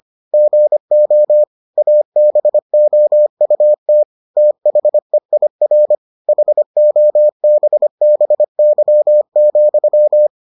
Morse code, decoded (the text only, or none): GO ABOUT THEIR HOBBY,